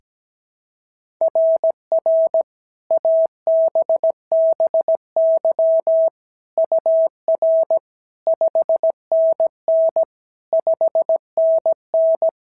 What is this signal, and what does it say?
Morse code, heard as a keyed tone, about 17 wpm, 650 Hz. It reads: RR ABBY UR 5NN 5NN
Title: SOTA QSO Practice 17wpm